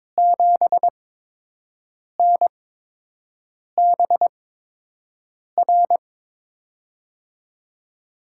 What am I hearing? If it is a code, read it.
Morse code, heard as a keyed tone, about 22 wpm, spaced as for 6 wpm, 700 Hz. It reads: 7NBR
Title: USA Call Signs - 6 Characters - 8x Character Spacing - 22wpm